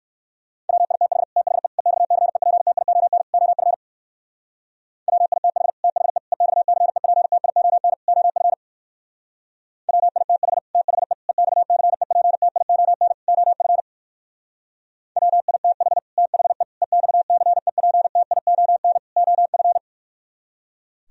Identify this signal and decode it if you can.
Morse code, heard as a keyed tone, about 58 wpm, 700 Hz. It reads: WITH THE EXCEPTION OF WITH THE EXCEPTION OF WITH THE EXCEPTION OF WITH THE EXCEPTION OF